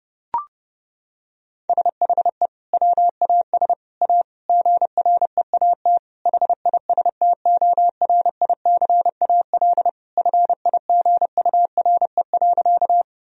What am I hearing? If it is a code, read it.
Morse code, heard as a keyed tone, about 30 wpm, 700 Hz. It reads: SHE WAS A GREAT HISTORICAL FIGURE.
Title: Sentences from Top 1000 Words - Review Part 4 - 30wpm